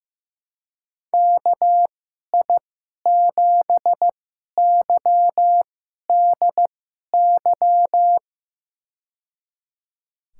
Morse code, heard as a keyed tone, about 15 wpm, spaced as for 11 wpm, 700 Hz. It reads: KI7YDY
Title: USA Call Signs - 6 Characters - 2x Character Spacing - 15wpm